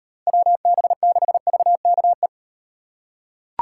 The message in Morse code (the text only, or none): WB6VKE